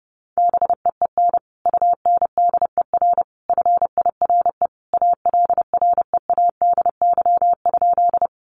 Morse code, heard as a keyed tone, about 30 wpm, 700 Hz. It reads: BEEN UNDER FIRE ALREADY?
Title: Sentences from Top 700 Words 30wpm